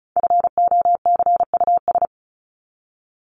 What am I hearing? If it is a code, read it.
Morse code, heard as a keyed tone, about 34 wpm, 700 Hz. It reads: FOCUS
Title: Top 501-750 Words 35wpm